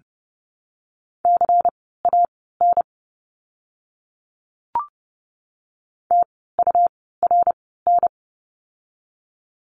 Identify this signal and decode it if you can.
Morse code, heard as a keyed tone, about 30 wpm, spaced as for 17 wpm, 700 Hz. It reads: CAN TURN
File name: course / 30:10wpm / 237 Lesson 38f - Letter q - Words 30wpm